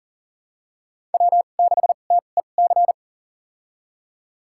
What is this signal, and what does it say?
Morse code, heard as a keyed tone, about 40 wpm, spaced as for 29 wpm, 700 Hz. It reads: W6TEC
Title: USA Call Signs - 5 Characters - 2x Character Spacing - 40wpm